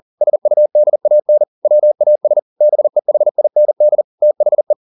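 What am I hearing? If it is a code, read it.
Morse code, heard as a keyed tone, about 40 wpm, 600 Hz. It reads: SUDAN WAS BEHIND THE
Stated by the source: Morse code at 40 wpm, practice text, CW med Gnister.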